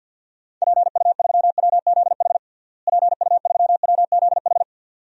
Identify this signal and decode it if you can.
Morse code, heard as a keyed tone, about 53 wpm, 700 Hz. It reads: PU3WZH PU3WZH
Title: DXCC-100 Call Signs - 6 Characters - Repeated 2x - 50wpm